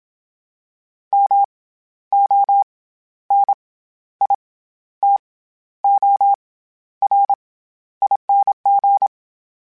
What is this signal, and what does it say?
Morse code, heard as a keyed tone, about 27 wpm, 800 Hz. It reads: M O N I T O R ING
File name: Special INGs-mp3s / Special INGs-mp3s / SlowToFaster ing @25